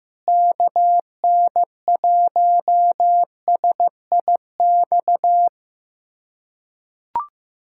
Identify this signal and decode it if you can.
Morse code, heard as a keyed tone, about 15 wpm, 700 Hz. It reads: KN1SIX
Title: USA Callsign DE Callsign - 6 Characters - 15wpm